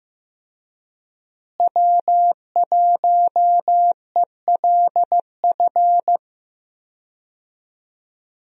Morse code, heard as a keyed tone, about 15 wpm, 700 Hz. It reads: W1ELF
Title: USA Call Signs - 5 Characters - Repeated 2x - 15wpm